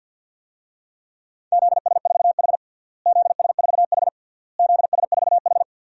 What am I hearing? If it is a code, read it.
Morse code, heard as a keyed tone, about 53 wpm, 700 Hz. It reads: ZS4H ZS4H ZS4H
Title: DXCC-100 Call Signs - 4 Characters - Repeated 3x - 50wpm